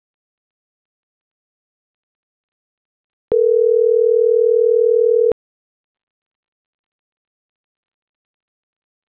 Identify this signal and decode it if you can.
Ringback tone (North America), 440 + 480 Hz.
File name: CENTRAL MABE / out saby.reyes25